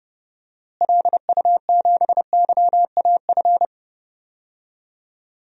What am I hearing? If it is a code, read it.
Morse code, heard as a keyed tone, about 29 wpm, 700 Hz. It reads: LU7YAF
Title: DXCC-100 Call Signs - 6 Characters - Repeated 3x - 30wpm